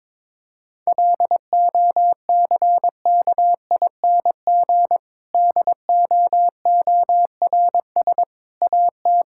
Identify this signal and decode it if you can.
Morse code, heard as a keyed tone, about 22 wpm, 700 Hz. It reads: LOCKING DOORS AT
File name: Difficult Sentences with Hints - 22wpm